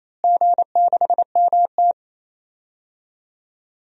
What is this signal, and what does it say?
Morse code, heard as a keyed tone, about 28 wpm, 700 Hz. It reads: G6MT
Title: DXCC-100 Call Signs - 4 Characters - Repeated 3x - 28wpm